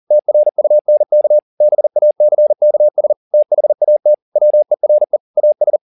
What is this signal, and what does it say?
Morse code, heard as a keyed tone, about 40 wpm, 600 Hz. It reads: TRUNK BACKS THAT WERE AS